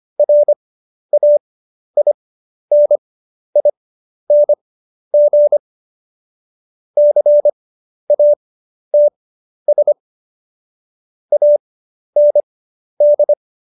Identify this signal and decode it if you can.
Morse code, heard as a keyed tone, about 25 wpm, spaced as for 11 wpm, 590 Hz. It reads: RAINING CATS AND
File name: Jokes-25-char-10-eff